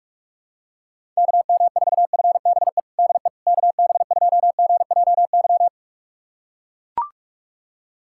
Morse code, heard as a keyed tone, about 41 wpm, 700 Hz. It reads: KM4FBE DE KB2GJY
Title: USA Callsign DE Callsign - 6 Characters - 45wpm